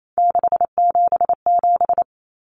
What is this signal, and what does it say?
Morse code, heard as a keyed tone, about 28 wpm, 700 Hz. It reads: 677
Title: Common Cut Numbers - 3-Digit 28wpm